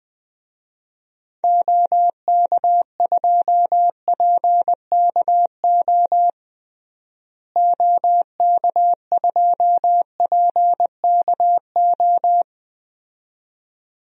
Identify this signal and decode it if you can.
Morse code, heard as a keyed tone, about 20 wpm, 700 Hz. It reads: OK2PKO OK2PKO
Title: DXCC-100 Call Signs - 6 Characters - Repeated 2x - 20wpm